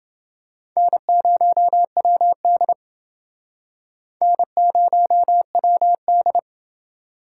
Morse code, heard as a keyed tone, about 29 wpm, 700 Hz. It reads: N0WD N0WD